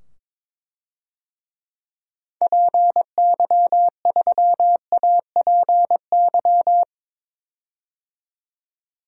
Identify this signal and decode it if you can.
Morse code, heard as a keyed tone, about 22 wpm, 700 Hz. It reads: PY3APY
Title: DXCC-100 Call Signs - 6 Characters - 22wpm